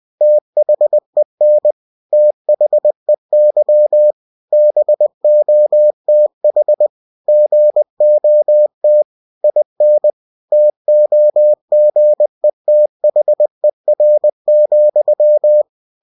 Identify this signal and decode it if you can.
Morse code, heard as a keyed tone, about 20 wpm, 600 Hz. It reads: THEN THEY BOTH GOT IN TOGETHER,
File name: cw-books / winnie-the-pooh / winnie-20wpm-page-0100